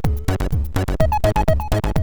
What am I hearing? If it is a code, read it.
There is no code.